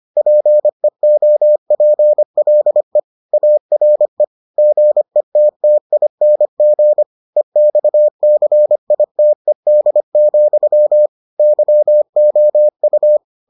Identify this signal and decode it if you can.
Morse code, heard as a keyed tone, about 25 wpm, 600 Hz. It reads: PEOPLE ARE GETTING EXCITED, YOU